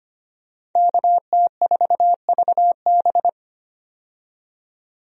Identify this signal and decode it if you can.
Morse code, heard as a keyed tone, about 25 wpm, 700 Hz. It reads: KT4VB
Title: USA Call Signs - 5 Characters - 25wpm